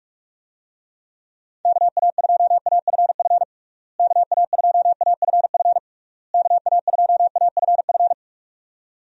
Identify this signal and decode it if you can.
Morse code, heard as a keyed tone, about 46 wpm, 700 Hz. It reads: KA2AFF KA2AFF KA2AFF
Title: USA Call Signs - 6 Characters - Repeated 3x - 45wpm